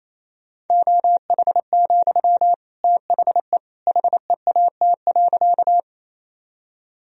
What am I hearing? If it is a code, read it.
Morse code, heard as a keyed tone, about 28 wpm, 700 Hz. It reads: OH, THE HEAT.
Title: Sentences from Top 1000 Words - Review Part 3 - 28wpm